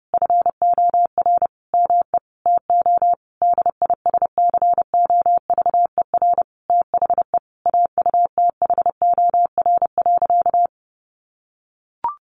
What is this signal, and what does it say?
Morse code, heard as a keyed tone, about 30 wpm, 700 Hz. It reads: FOR ME TO DISCOVER THE AUTHOR.
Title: Sentences from Top 1200 Words 30wpm